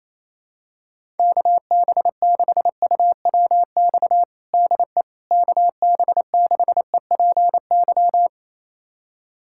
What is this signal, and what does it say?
Morse code, heard as a keyed tone, about 28 wpm, 700 Hz. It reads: KB6UWX DE KB6EPY
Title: USA Callsign DE Callsign 28wpm